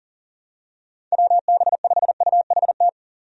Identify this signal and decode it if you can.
Morse code, heard as a keyed tone, about 37 wpm, 700 Hz. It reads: WB5UHT
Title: USA Call Signs - 6 Characters - 40wpm